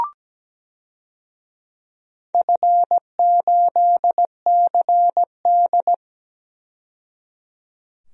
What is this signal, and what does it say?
Morse code, heard as a keyed tone, about 17 wpm, 700 Hz. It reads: F8CD
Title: DXCC-100 Call Signs - 4 Characters - 17wpm